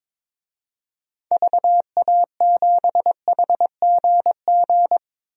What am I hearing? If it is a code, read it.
Morse code, heard as a keyed tone, about 22 wpm, 700 Hz. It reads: VA7HGG
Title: DXCC-100 Call Signs - 6 Characters - 22wpm